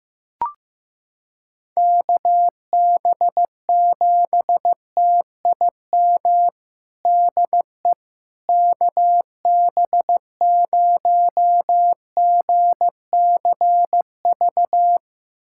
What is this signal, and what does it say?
Morse code, heard as a keyed tone, about 15 wpm, 700 Hz. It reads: KB7TIM DE KB0GCV
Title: USA Callsign DE Callsign - 6 Characters - 15wpm